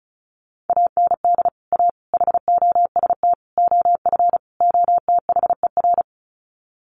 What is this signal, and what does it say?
Morse code, heard as a keyed tone, about 34 wpm, 700 Hz. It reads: AND A HOST OF OTHER
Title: Sets of 5 Words Encore2 35wpm